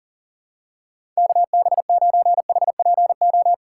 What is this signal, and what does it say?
Morse code, heard as a keyed tone, about 40 wpm, 700 Hz. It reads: KB9HPO
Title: USA Call Signs - 6 Characters - 6x Character Spacing - 40wpm